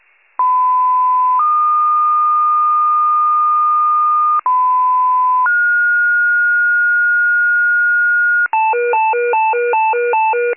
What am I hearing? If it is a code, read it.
Two-tone page, heard as a one-second tone, then a three-second tone, about 1010 then 1220 Hz, about 1010 then 1430 Hz, with noise. It is heard from 0.4 to 8.5 s.